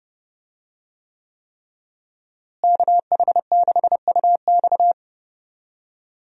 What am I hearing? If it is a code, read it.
Morse code, heard as a keyed tone, about 30 wpm, 700 Hz. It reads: KH6UX
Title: USA Call Signs - 5 Characters - Repeated 3x - 30wpm